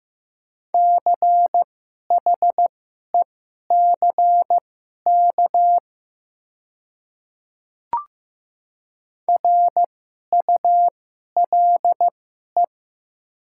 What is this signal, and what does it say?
Morse code, heard as a keyed tone, about 15 wpm, spaced as for 11 wpm, 700 Hz. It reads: CHECK RULE
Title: 2x Character Spacing - Top 500 Words - 15wpm